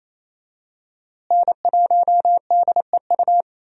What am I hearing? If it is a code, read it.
Morse code, heard as a keyed tone, about 27 wpm, 700 Hz. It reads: N1DEU
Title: USA Call Signs - 5 Characters - Repeated 2x - 28wpm